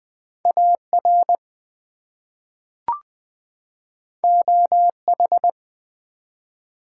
Morse code, heard as a keyed tone, about 20 wpm, 700 Hz. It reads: AR OH
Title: ARRL Sections v2 - 20wpm